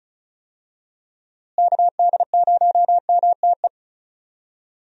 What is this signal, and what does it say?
Morse code, heard as a keyed tone, about 34 wpm, 700 Hz. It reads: KD0MTE
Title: USA Call Signs - 6 Characters - Repeated 2x - 35wpm